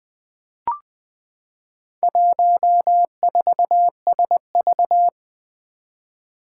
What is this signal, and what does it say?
Morse code, heard as a keyed tone, about 20 wpm, 700 Hz. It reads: 14SV
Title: Four Challenging Letter-Number Combinations 20wpm